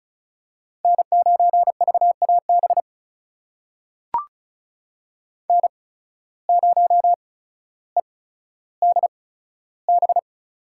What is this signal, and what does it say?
Morse code, heard as a keyed tone, about 36 wpm, spaced as for 15 wpm, 700 Hz. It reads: N9VAB N0EDB
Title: USA Call Signs - 5 Characters - 8x Character Spacing - 35wpm